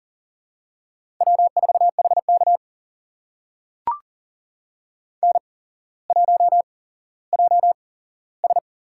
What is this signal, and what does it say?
Morse code, heard as a keyed tone, about 40 wpm, spaced as for 17 wpm, 700 Hz. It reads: W4HK N1JS
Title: USA Call Signs - 8x Character Spacing - 40wpm